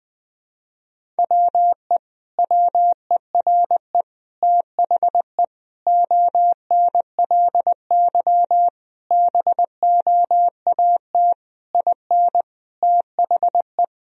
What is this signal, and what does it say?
Morse code, heard as a keyed tone, about 20 wpm, 700 Hz. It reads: WE WERE THE ONLY BOAT IN THE
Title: Sentences from Top 1000 Words - Review Part 1 - 20wpm